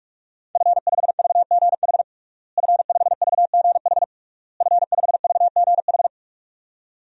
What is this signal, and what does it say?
Morse code, heard as a keyed tone, about 46 wpm, 700 Hz. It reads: F5VGH F5VGH F5VGH
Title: DXCC-100 Call Signs - 5 Characters - Repeated 3x - 45wpm